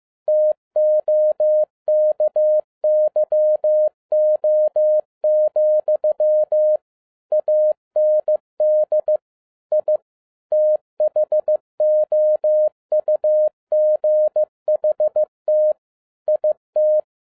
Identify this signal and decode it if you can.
Morse code, heard as a keyed tone, about 15 wpm, 610 Hz. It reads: TOKYO, AND I THOUGHT IT
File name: 2011-07-mp3 / n47 JOHN K3TN 15wpm